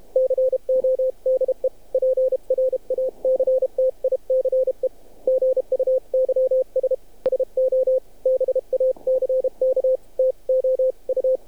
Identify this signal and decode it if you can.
Morse code, heard as a keyed tone, about 30 wpm, 520 Hz. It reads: CODE PRACTICE GUYS SO BACK TO U
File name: archived / CRSnet